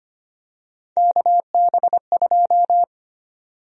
Morse code, heard as a keyed tone, about 25 wpm, 700 Hz. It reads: KB2
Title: USA Call Sign Prefixes - 25wpm